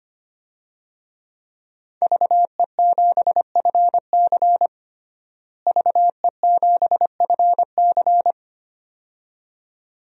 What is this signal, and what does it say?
Morse code, heard as a keyed tone, about 25 wpm, 700 Hz. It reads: VE7FC VE7FC
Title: DXCC-100 Call Signs - 5 Characters - Repeated 2x - 25wpm